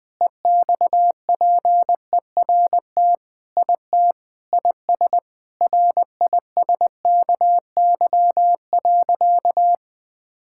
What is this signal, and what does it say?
Morse code, heard as a keyed tone, about 20 wpm, 700 Hz. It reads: EXPERT IT IS RISKY.
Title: Sentences with 6-Word Phrases - Spoken Beginning - 20wpm